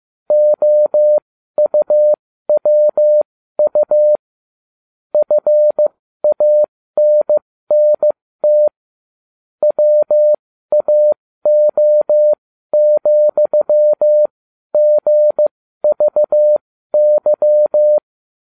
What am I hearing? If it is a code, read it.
Morse code, heard as a keyed tone, about 15 wpm, spaced as for 12 wpm, 600 Hz. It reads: OUWU FANNT WAO,GVY